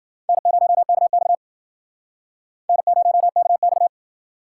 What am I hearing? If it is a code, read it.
Morse code, heard as a keyed tone, about 52 wpm, 700 Hz. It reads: N0KX N0KX